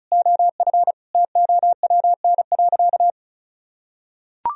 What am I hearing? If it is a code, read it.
Morse code, heard as a keyed tone, about 36 wpm, 700 Hz. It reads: OF TOWN.
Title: Sentences from Top 1000 Words - Review Part 2 - 35wpm